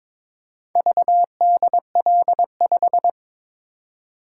Morse code, heard as a keyed tone, about 22 wpm, 700 Hz. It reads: VDL5
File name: practice / 22wpm / Four Letter-Number Combinations - 22wpm